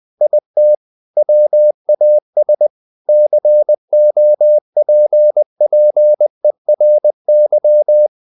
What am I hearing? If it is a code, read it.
Morse code, heard as a keyed tone, about 20 wpm, 600 Hz. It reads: IT WAS COPPERY
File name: the old man and the sea part 2 20wpm